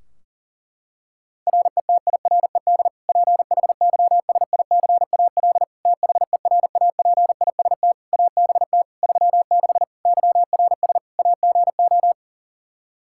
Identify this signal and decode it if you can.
Morse code, heard as a keyed tone, about 40 wpm, 700 Hz. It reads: RETIRED PHYSICAL THERAPIST ABT 36 YRS AGO
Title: Practice QSO - Exchange 3 - More Info - 40wpm